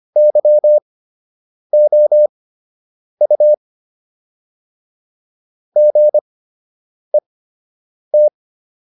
Morse code, heard as a keyed tone, about 25 wpm, spaced as for 8 wpm, 600 Hz. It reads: YOU GET